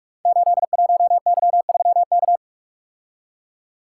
Morse code, heard as a keyed tone, about 44 wpm, 700 Hz. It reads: 81Y3K
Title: Code Groups - 5 Letter:Numbers - 45wpm